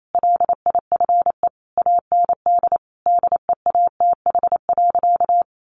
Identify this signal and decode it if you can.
Morse code, heard as a keyed tone, about 28 wpm, 700 Hz. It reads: LIFE AND DEATH.
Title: Sentences from Top 1100 Words 28wpm